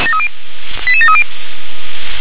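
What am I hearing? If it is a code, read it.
Five-tone selective call (ZVEI), heard as quick runs of five tones, tones of about 70 ms. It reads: 88730
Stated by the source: ZVEI five-tone selective call 88148, received on an SDR.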